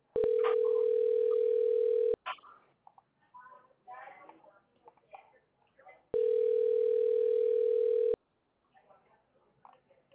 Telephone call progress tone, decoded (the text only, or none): ringback tone